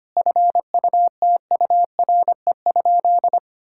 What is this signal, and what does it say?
Morse code, heard as a keyed tone, about 25 wpm, 700 Hz. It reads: FUTURE?
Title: Sentences from Top 1000 Words - Review Part 6 - 25wpm